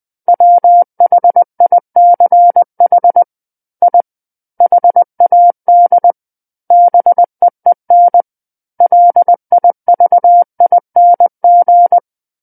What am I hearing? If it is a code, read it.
Morse code, heard as a keyed tone, about 20 wpm, 700 Hz. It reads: WHICH I HAD BEEN LIVING
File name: 20 WPM